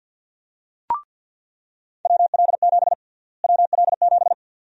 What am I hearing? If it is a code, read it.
Morse code, heard as a keyed tone, about 49 wpm, 700 Hz. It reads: WL7 WL7
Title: USA Call Sign Prefixes - Repeated 2x - 50wpm